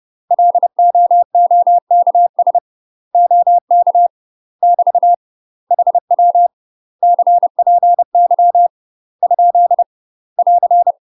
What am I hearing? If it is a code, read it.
Morse code, heard as a keyed tone, about 30 wpm, 700 Hz. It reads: LOOKS OK <BT> HW CPY ? <AR>
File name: UK MASTER QSOs TRAINING / 30WPM / UK MASTER QSOs 0033